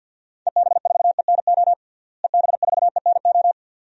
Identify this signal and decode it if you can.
Morse code, heard as a keyed tone, about 46 wpm, 700 Hz. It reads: EB4ENO EB4ENO
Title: DXCC-100 Call Signs - 6 Characters - Repeated 2x - 50wpm